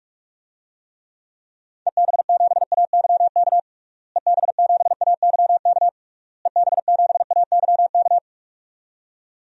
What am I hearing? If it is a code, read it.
Morse code, heard as a keyed tone, about 46 wpm, 700 Hz. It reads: EB7AYK EB7AYK EB7AYK
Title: DXCC-100 Call Signs - 6 Characters - Repeated 3x - 45wpm